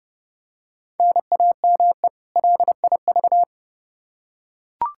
Morse code, heard as a keyed tone, about 30 wpm, 700 Hz. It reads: NAME LIV
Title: Practice QSO - Exchange 1 - Beginning - 30wpm